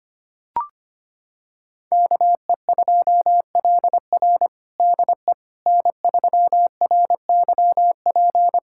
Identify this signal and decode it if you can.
Morse code, heard as a keyed tone, about 25 wpm, 700 Hz. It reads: KE2LR DE N3RYP
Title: USA Callsign DE Callsign - 5 Characters - 25wpm